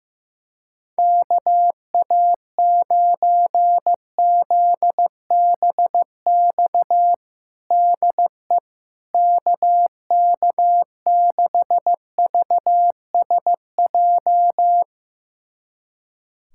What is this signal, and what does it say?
Morse code, heard as a keyed tone, about 15 wpm, 700 Hz. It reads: KA9ZBX DE KK6VSJ
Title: USA Callsign DE Callsign - 6 Characters - 15wpm